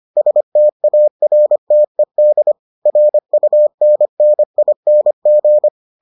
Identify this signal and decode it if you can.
Morse code, heard as a keyed tone, about 25 wpm, 600 Hz. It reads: STARTED RUNNING